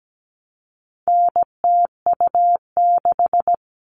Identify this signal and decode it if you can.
Morse code, heard as a keyed tone, about 17 wpm, 700 Hz. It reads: NTU6
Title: All Cut Numbers - 4-Digit 17wpm